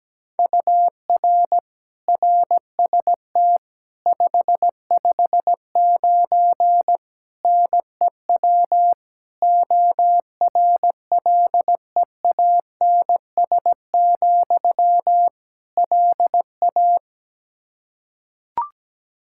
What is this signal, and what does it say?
Morse code, heard as a keyed tone, about 17 wpm, 700 Hz. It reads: UR RST 559 NEW ORLEANS, LA
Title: Practice QSO - Exchange 1 - Beginning - 17wpm